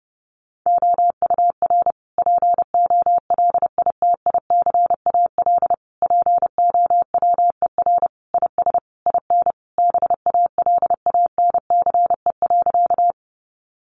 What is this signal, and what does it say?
Morse code, heard as a keyed tone, about 30 wpm, 700 Hz. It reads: OUR POLITICAL POWER IS IN BALANCE.
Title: Sentences from Top 900 Words 30wpm